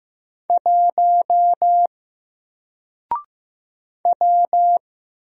Morse code, heard as a keyed tone, about 15 wpm, 700 Hz. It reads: 1 W
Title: Single Letter-Number Alt 15wpm